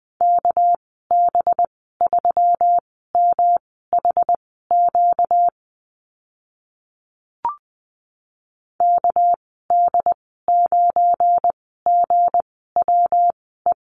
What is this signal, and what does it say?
Morse code, heard as a keyed tone, about 20 wpm, spaced as for 14 wpm, 700 Hz. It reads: KB3MHQ KD9GWE
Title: 238 Lesson 38g - Letter q - Call Signs 20wpm